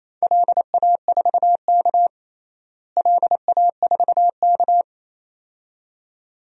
Morse code, heard as a keyed tone, about 27 wpm, 700 Hz. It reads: LA4K LA4K
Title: DXCC-100 Call Signs - 4 Characters - Repeated 2x - 28wpm